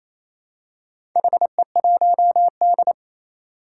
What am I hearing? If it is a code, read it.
Morse code, heard as a keyed tone, about 28 wpm, 700 Hz. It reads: HE1D
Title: Four Challenging Letter-Number Combinations 28wpm